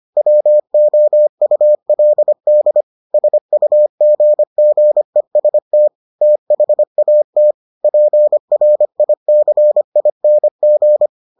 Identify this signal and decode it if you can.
Morse code, heard as a keyed tone, about 25 wpm, 600 Hz. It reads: WOULD SUGGEST THAT PRICING